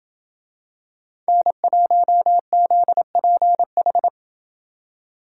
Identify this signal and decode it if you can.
Morse code, heard as a keyed tone, about 28 wpm, 700 Hz. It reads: N1ZPH